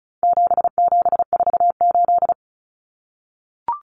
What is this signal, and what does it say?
Morse code, heard as a keyed tone, about 34 wpm, 700 Hz. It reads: 7748
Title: Common Cut Numbers - 4-Digit 35wpm